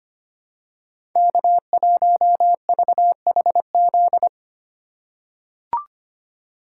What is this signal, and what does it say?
Morse code, heard as a keyed tone, about 25 wpm, 700 Hz. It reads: K1VHZ
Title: USA Call Signs - 5 Characters - 2x Character Spacing - 25wpm